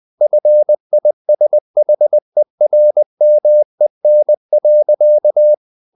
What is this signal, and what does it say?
Morse code, heard as a keyed tone, about 20 wpm, 600 Hz. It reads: FISHERMEN.
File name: the old man and the sea / the old man and the sea part 1 20wpm